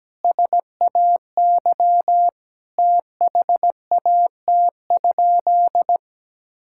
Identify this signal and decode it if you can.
Morse code, heard as a keyed tone, about 17 wpm, 700 Hz. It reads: SAY THAT?
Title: Sentences from Top 1000 Words - Review Part 5 - 17wpm